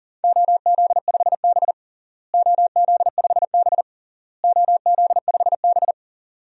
Morse code, heard as a keyed tone, about 40 wpm, 700 Hz. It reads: OZ5B OZ5B OZ5B
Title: DXCC-100 Call Signs - 4 Characters - Repeated 3x - 40wpm